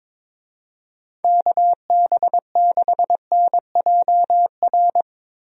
Morse code, heard as a keyed tone, about 21 wpm, 700 Hz. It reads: KB6NJR
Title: USA Call Signs - 6 Characters - Repeated 3x - 22wpm